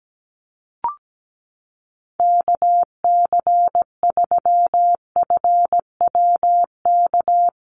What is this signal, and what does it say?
Morse code, heard as a keyed tone, about 17 wpm, 700 Hz. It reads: KC3FWK
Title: USA Call Signs Repeated 2x 17wpm